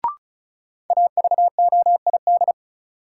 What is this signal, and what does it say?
Morse code, heard as a keyed tone, about 34 wpm, 700 Hz. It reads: AVOID
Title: Sentences from Top 1200 Words - New Word Drill Alt - 35wpm